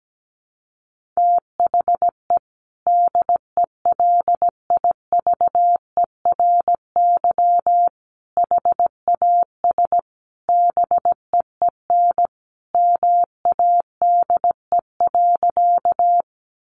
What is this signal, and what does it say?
Morse code, heard as a keyed tone, about 17 wpm, 700 Hz. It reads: THE DELIVERY HAS BEEN MADE.
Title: Sentences from Top 1500 Words 17wpm